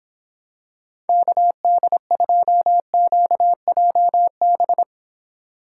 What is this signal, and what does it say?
Morse code, heard as a keyed tone, about 26 wpm, 700 Hz. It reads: KD2QJB